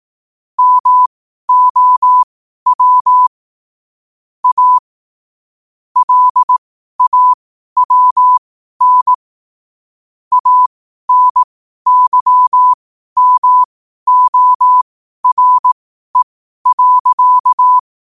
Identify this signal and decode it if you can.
Morse code, heard as a keyed tone, about 18 wpm, spaced as for 12 wpm, 1000 Hz. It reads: MOW A LAWN ANYMORE.
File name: qotdfiles / 12wpm